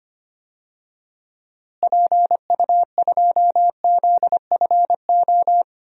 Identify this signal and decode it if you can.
Morse code, heard as a keyed tone, about 25 wpm, 700 Hz. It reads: PU2ZFO
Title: DXCC-100 Call Signs - 6 Characters - Repeated 2x - 25wpm